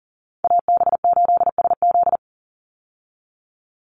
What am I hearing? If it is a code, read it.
Morse code, heard as a keyed tone, about 39 wpm, 700 Hz. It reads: AB8SZ
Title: USA Call Signs 40wpm